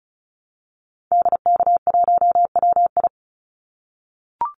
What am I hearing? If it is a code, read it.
Morse code, heard as a keyed tone, about 36 wpm, 700 Hz. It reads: DK1WI
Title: Call Signs - 35wpm